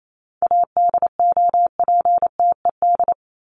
Morse code, heard as a keyed tone, about 28 wpm, 700 Hz. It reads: ADOPTED
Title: ED Suffix - Top 100 Words with Base Words 28wpm